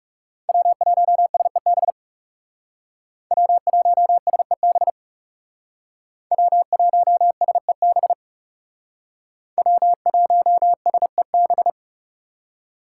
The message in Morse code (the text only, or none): W1SEB W1SEB W1SEB W1SEB